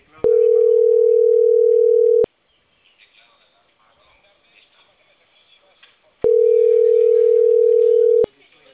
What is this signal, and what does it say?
Ringback tone (North America), 440 + 480 Hz, with speech.
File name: CENTRAL CAEA / out agcaea02 - .18792